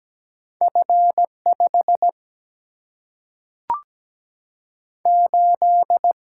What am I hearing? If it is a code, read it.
Morse code, heard as a keyed tone, about 17 wpm, 700 Hz. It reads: F 5 8
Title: Single-Letter Single-Number 17wpm